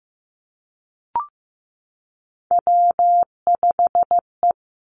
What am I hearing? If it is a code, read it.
Morse code, heard as a keyed tone, about 15 wpm, 700 Hz. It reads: W5E
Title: Three Challenging Letter-Number Combinations 15wpm